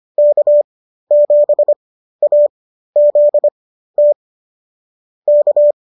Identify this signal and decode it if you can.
Morse code, heard as a keyed tone, about 25 wpm, spaced as for 13 wpm, 570 Hz. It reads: K7AZT K